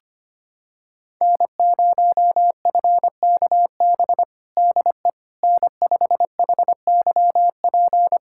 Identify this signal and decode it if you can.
Morse code, heard as a keyed tone, about 25 wpm, 700 Hz. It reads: N0FKB DE N5HYP
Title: USA Callsign DE Callsign - 5 Characters - 25wpm